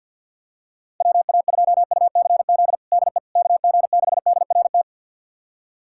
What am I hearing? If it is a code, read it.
Morse code, heard as a keyed tone, about 47 wpm, 700 Hz. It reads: WA2UCZ DE KG6DRT
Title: USA Callsign DE Callsign - 6 Characters - 50wpm